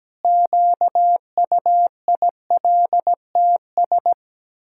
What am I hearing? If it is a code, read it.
Morse code, heard as a keyed tone, about 17 wpm, 700 Hz. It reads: QUILTS
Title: Pangram Challenge v2 - 17wpm